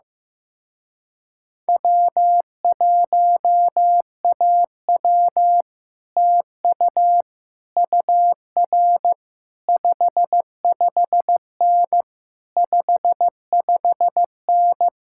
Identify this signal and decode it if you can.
Morse code, heard as a keyed tone, about 15 wpm, 700 Hz. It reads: W1AW TU UR 55N 55N
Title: Speed-Racing Common QSO Phrases Encore - 15wpm